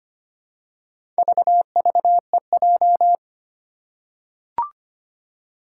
Morse code, heard as a keyed tone, about 25 wpm, 700 Hz. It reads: VVEJ
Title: Four Challenging Letter-Number Combinations 25wpm